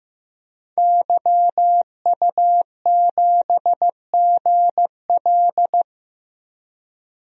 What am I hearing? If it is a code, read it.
Morse code, heard as a keyed tone, about 15 wpm, 700 Hz. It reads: YU7GL
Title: DXCC-100 Call Signs - 5 Characters - 15wpm